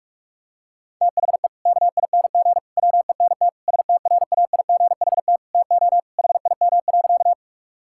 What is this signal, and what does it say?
Morse code, heard as a keyed tone, about 46 wpm, 700 Hz. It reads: THE KING WENT STRAIGHT TO HIM.
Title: Sentences from Top 1000 Words - Review Part 6 - 45wpm